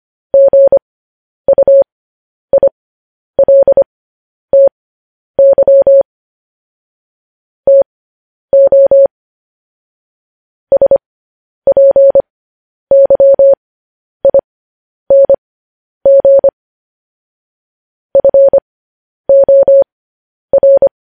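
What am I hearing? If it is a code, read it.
Morse code, heard as a keyed tone, about 25 wpm, spaced as for 10 wpm, 550 Hz. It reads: GUILTY TO SPYING FOR
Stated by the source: Morse news headlines at 10.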